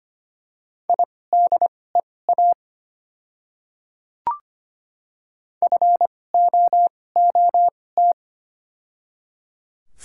Morse code, heard as a keyed tone, about 25 wpm, spaced as for 18 wpm, 700 Hz. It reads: IDEA FOOT
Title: 2x Character Spacing - Top 500 Words - 25wpm